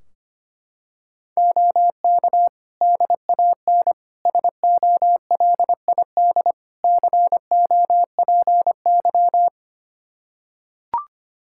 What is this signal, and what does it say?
Morse code, heard as a keyed tone, about 25 wpm, 700 Hz. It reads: OK DAN SOLID COPY
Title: Practice QSO - Exchange 3 - More Info - 25wpm